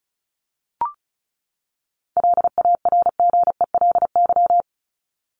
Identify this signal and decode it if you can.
Morse code, heard as a keyed tone, about 34 wpm, 700 Hz. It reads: LARGELY
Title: LY Suffix - Top 100 Words with Base Words 35wpm